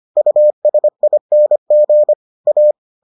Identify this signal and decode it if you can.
Morse code, heard as a keyed tone, about 25 wpm, 600 Hz. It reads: USING A